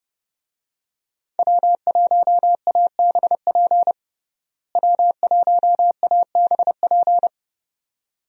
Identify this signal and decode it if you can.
Morse code, heard as a keyed tone, about 30 wpm, 700 Hz. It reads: W1ABP W1ABP
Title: USA Call Signs - 5 Characters - Repeated 2x - 30wpm